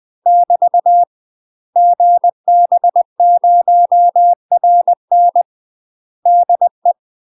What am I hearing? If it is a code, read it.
Morse code, heard as a keyed tone, about 20 wpm, 700 Hz. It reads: <BT> GB0RN DE
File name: UK MASTER QSOs TRAINING / 20WPM / UK MASTER QSOs 0038